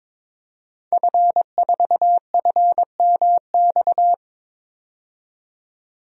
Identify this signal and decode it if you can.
Morse code, heard as a keyed tone, about 22 wpm, 700 Hz. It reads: F4FMX
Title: DXCC-100 Call Signs - 5 Characters - 22wpm